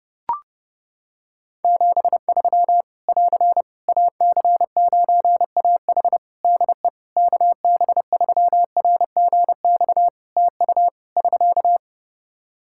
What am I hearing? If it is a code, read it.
Morse code, heard as a keyed tone, about 28 wpm, 700 Hz. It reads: 73 <AR> AC9AH DE KB3RGX TU <SK>
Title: Practice QSO - Exchange 4 - Ending - 30wpm